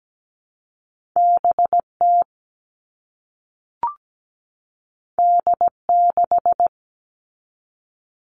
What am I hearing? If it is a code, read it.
Morse code, heard as a keyed tone, about 17 wpm, 700 Hz. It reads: BT D6
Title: All Cut Numbers - 2-Digit 17wpm